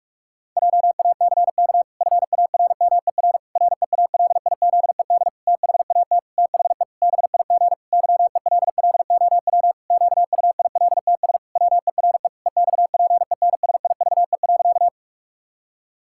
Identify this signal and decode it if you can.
Morse code, heard as a keyed tone, about 46 wpm, 700 Hz. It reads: JACK FARMER REALIZED THAT THE BIG YELLOW QUILTS WERE EXPENSIVE.
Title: Pangram Challenge v2 - 45wpm